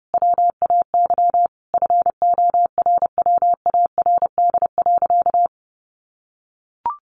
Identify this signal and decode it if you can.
Morse code, heard as a keyed tone, about 30 wpm, 700 Hz. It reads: WAY FORWARD.
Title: Sentences from Top 1500 Words 30wpm